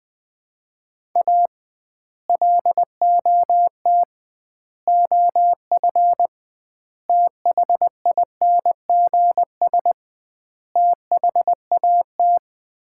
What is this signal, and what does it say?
Morse code, heard as a keyed tone, about 20 wpm, 700 Hz. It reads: A LOT OF THINGS THAT
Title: 2x Word-Spacing - Sets of 5 Words - 20wpm